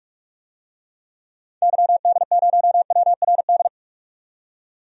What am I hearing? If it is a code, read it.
Morse code, heard as a keyed tone, about 43 wpm, 700 Hz. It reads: YD0WRD
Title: DXCC-100 Call Signs - 6 Characters - 45wpm